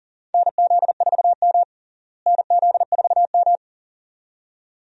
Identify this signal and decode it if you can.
Morse code, heard as a keyed tone, about 37 wpm, 700 Hz. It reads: NZ4M NZ4M
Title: USA Call Signs - 4 Characters - Repeated 2x - 40wpm